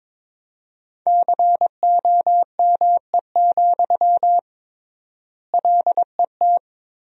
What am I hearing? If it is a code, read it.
Morse code, heard as a keyed tone, about 22 wpm, 700 Hz. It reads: COME, LET
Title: 3x Word-Spacing - Sentences from Top 300 Words - 22wpm